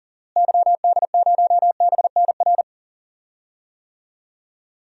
Morse code, heard as a keyed tone, about 40 wpm, 700 Hz. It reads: YD0BNR
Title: DXCC-100 Call Signs - 6 Characters - Repeated 3x - 40wpm